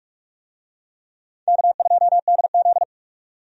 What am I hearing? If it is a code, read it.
Morse code, heard as a keyed tone, about 46 wpm, 700 Hz. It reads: K2DZ
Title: USA Call Signs - 4 Characters - 45wpm